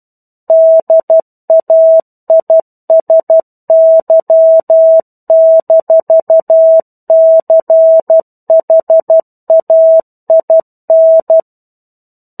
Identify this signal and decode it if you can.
Morse code, heard as a keyed tone, about 12 wpm, 650 Hz. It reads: DAISY-CHAIN
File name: 01-Chapter-01 12 WPM